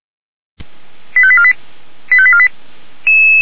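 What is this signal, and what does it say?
Five-tone selective call (ZVEI), heard as quick runs of five tones, tones of about 70 ms. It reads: 85848 85848
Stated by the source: ZVEI five-tone selective call received on an SDR.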